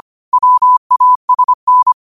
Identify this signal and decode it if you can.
Morse code, heard as a keyed tone, about 25 wpm, 1000 Hz. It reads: WASN